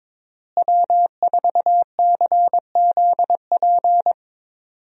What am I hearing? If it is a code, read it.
Morse code, heard as a keyed tone, about 22 wpm, 700 Hz. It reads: W4CZP
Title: USA Call Signs - 5 Characters - Repeated 3x - 22wpm